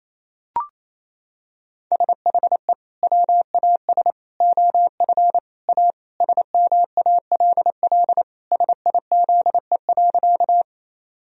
Morse code, heard as a keyed tone, about 28 wpm, 700 Hz. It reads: SHE WAS OF A SMALL SIZE.
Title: Sentences from Top 1000 Words - Review Part 2 - 28wpm